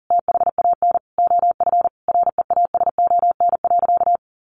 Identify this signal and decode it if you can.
Morse code, heard as a keyed tone, about 40 wpm, 700 Hz. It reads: THAN OF REASON.
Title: Sentences from Top 1300 Words 40wpm